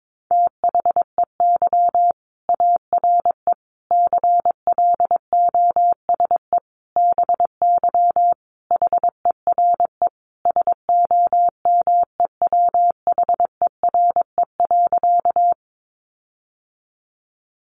Morse code, heard as a keyed tone, about 22 wpm, 700 Hz. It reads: THEY ARE CLOSE BY HERE SOMEWHERE.
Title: Sentences from Top 1400 Words 22wpm